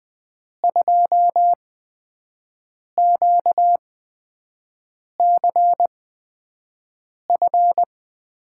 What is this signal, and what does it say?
Morse code, heard as a keyed tone, about 20 wpm, spaced as for 5 wpm, 700 Hz. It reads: 2QCF
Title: USA Call Signs - 6 Characters - 8x Character Spacing - 20wpm